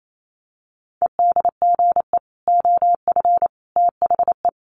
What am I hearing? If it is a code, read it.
Morse code, heard as a keyed tone, about 28 wpm, 700 Hz. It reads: EDGE OF THE
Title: Sets of 3 Words 28wpm